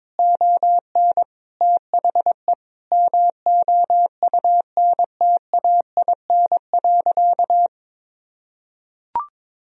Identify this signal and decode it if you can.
Morse code, heard as a keyed tone, about 22 wpm, 700 Hz. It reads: ON THE MOUNTAIN.
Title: Sentences from Top 1000 Words - Review Part 1 - 22wpm